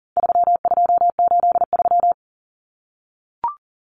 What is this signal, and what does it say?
Morse code, heard as a keyed tone, about 40 wpm, 700 Hz. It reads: 3283
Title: Common Cut Numbers - 4-Digit 40wpm